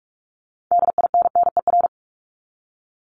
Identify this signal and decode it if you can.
Morse code, heard as a keyed tone, about 44 wpm, 700 Hz. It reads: DINNER
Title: Top 1751-2000 Words 45wpm